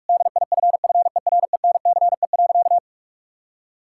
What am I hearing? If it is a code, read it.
Morse code, heard as a keyed tone, about 43 wpm, 700 Hz. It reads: DIFFERENCE.
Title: Sentences from Top 1000 Words - Review Part 5 - 45wpm